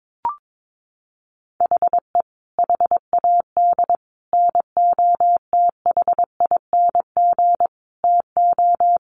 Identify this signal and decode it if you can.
Morse code, heard as a keyed tone, about 22 wpm, 700 Hz. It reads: HE HAD NOTHING TO
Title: Sentences from Top 1300 Words 22wpm